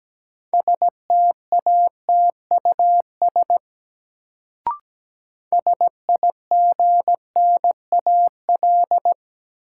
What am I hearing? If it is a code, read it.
Morse code, heard as a keyed tone, about 17 wpm, 700 Hz. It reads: STATUS SIGNAL
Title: Sentences from Top 1400 Words - New Word Drill - 17wpm